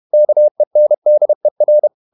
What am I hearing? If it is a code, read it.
Morse code, heard as a keyed tone, about 31 wpm, 600 Hz. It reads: KENDER
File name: Ferie QRQ 31wpm